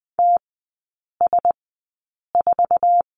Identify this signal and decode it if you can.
Morse code, heard as a keyed tone, about 20 wpm, spaced as for 8 wpm, 700 Hz. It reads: TS4
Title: 067 Lesson 12e - Letter d - Three Characters 20wpm